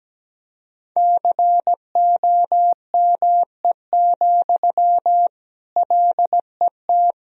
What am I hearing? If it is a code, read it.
Morse code, heard as a keyed tone, about 17 wpm, 700 Hz. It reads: COME, LET
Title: Sentences from Top 500 Words and Abbreviations 17wpm